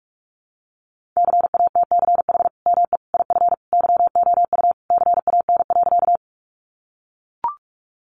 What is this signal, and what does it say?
Morse code, heard as a keyed tone, about 44 wpm, 700 Hz. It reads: CATCH ME IF YOU CAN.
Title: Sentences from Top 1500 Words 45wpm